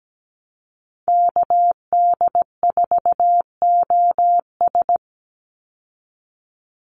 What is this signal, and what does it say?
Morse code, heard as a keyed tone, about 17 wpm, 700 Hz. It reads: KD4OS
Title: Call Signs Repeated 3x - 17wpm